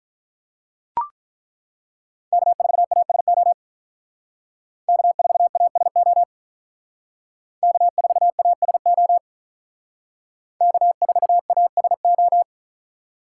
Morse code, heard as a keyed tone, about 42 wpm, 700 Hz. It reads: K4ASO K4ASO K4ASO K4ASO
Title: Speed Racing - USA 1x3 Call Signs 35wpm